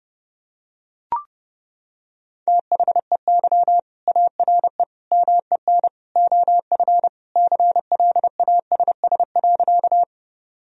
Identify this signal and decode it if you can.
Morse code, heard as a keyed tone, about 30 wpm, 700 Hz. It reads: THEY ARE MEN OF CLASS.
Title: Sentences from Top 1000 Words - Review Part 2 - 30wpm